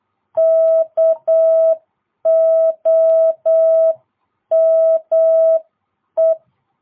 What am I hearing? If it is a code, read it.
Morse code, heard as a keyed tone, about 8 wpm, 660 Hz. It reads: KOME